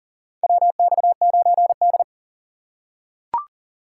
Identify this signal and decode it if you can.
Morse code, heard as a keyed tone, about 40 wpm, 700 Hz. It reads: WX9D
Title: USA Call Signs - 4 Characters - 4x Character Spacing - 40wpm